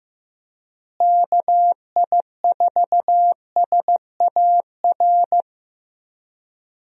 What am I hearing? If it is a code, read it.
Morse code, heard as a keyed tone, about 15 wpm, 700 Hz. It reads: KI4SAR
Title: USA Call Signs - 6 Characters - Repeated 3x - 15wpm